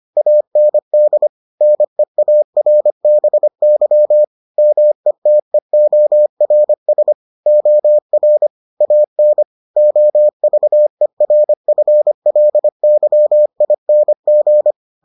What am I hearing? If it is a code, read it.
Morse code, heard as a keyed tone, about 25 wpm, 600 Hz. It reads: AND NEARBY METEORS OR AN OVERFLYING